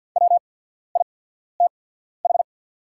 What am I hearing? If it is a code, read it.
Morse code, heard as a keyed tone, about 50 wpm, spaced as for 14 wpm, 700 Hz. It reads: WITH
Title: 8x Character Spacing - Top 100 Words - 50wpm